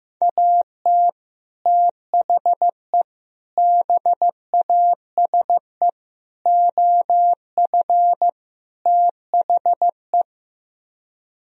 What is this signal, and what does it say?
Morse code, heard as a keyed tone, about 15 wpm, 700 Hz. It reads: AT THE BASE OF THE
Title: Speed-Racing - Sets of 5 Words 15wpm